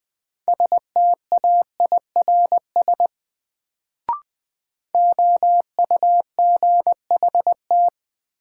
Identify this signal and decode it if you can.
Morse code, heard as a keyed tone, about 20 wpm, 700 Hz. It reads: STAIRS OUGHT
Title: Sets of 5 Words Encore - New Word Drill - 20wpm